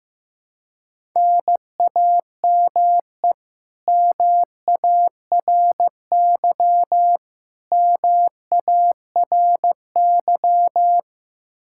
Morse code, heard as a keyed tone, about 16 wpm, 700 Hz. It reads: NAME MARY MARY
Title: K1USN SST QSO Practice 15wpm